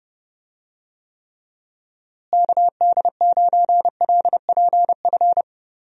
Morse code, heard as a keyed tone, about 30 wpm, 700 Hz. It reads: KD9LPF
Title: USA Call Signs - 6 Characters - Repeated 2x - 30wpm